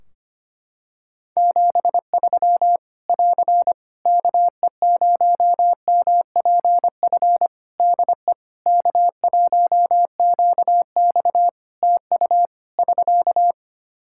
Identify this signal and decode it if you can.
Morse code, heard as a keyed tone, about 25 wpm, 700 Hz. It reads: 73 <AR> KE0MPF DE K1QX TU <SK>
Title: Practice QSO - Exchange 4 - Ending - 25wpm